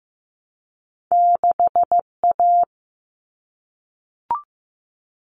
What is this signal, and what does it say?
Morse code, heard as a keyed tone, about 15 wpm, 700 Hz. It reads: 6A
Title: All Cut Numbers - 2-Digit 15wpm